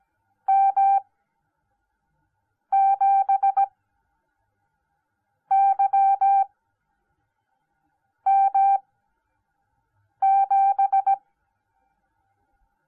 Morse code, heard as a keyed tone, about 17 wpm, spaced as for 5 wpm, 790 Hz. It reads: M7YM7